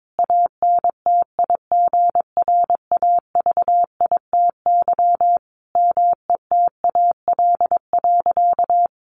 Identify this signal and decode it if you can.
Morse code, heard as a keyed tone, about 22 wpm, 700 Hz. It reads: ANTIGRAVITY METAL.
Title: A Sound Of Thunder 22wpm